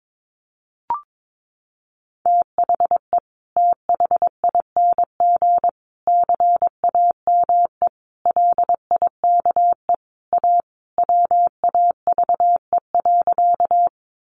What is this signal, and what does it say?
Morse code, heard as a keyed tone, about 22 wpm, 700 Hz. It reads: THE THING CAME LIKE A WAVE.
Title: Sentences from Top 1400 Words 22wpm